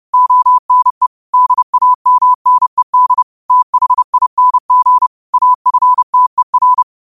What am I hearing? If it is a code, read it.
Morse code, heard as a keyed tone, about 30 wpm, 1000 Hz. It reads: ONE DAMNED THING AFTER
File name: qotdfiles / 30wpm